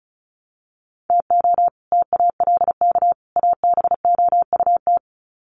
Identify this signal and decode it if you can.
Morse code, heard as a keyed tone, about 34 wpm, 700 Hz. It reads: TO TALK ABOUT
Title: Sets of 3 Words 35wpm